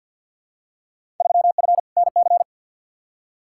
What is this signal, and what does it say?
Morse code, heard as a keyed tone, about 51 wpm, 700 Hz. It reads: 3F NC
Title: Field Day Exchanges - 50wpm